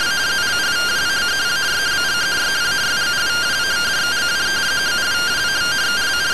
Two-tone teleprinter signal, about 170 Hz shift.